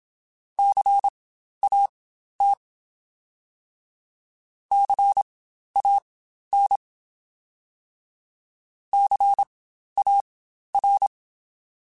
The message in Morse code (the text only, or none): CAT CAN CAR